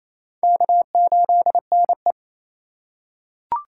Morse code, heard as a keyed tone, about 28 wpm, 700 Hz. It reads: K8NE
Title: USA Call Signs - 4x Character Spacing - 28wpm